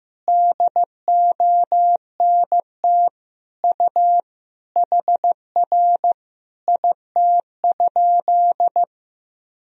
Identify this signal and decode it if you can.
Morse code, heard as a keyed tone, about 15 wpm, 700 Hz. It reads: DONT U HR IT?
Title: Sentences from Top 300 Words and Abbreviations 15wpm